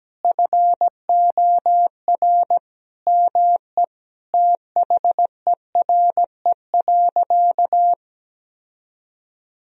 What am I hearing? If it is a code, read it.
Morse code, heard as a keyed tone, about 17 wpm, 700 Hz. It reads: FOR ME THERE.
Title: Sentences from Top 1000 Words - Review Part 4 - 17wpm